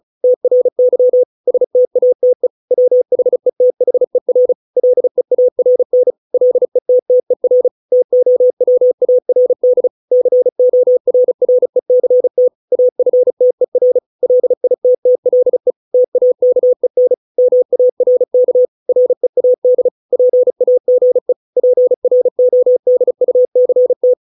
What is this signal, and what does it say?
Morse code, heard as a keyed tone, about 35 wpm, 500 Hz. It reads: TRY STATE WHETHER LEARN LETTER TOWARD CORRECT AFTER LITTLE TAKEN MARK READ PAGE PRODUCT